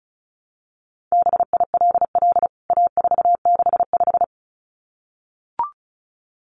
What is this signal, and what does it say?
Morse code, heard as a keyed tone, about 34 wpm, 700 Hz. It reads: BILL A465
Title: CWT Exchange 35wpm